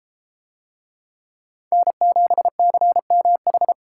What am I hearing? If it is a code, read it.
Morse code, heard as a keyed tone, about 30 wpm, 700 Hz. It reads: N7CMH